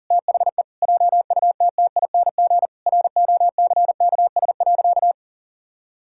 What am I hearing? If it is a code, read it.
Morse code, heard as a keyed tone, about 40 wpm, 700 Hz. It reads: THE JUTTING ROCKS.
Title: Pangram Challenge v2 - 40wpm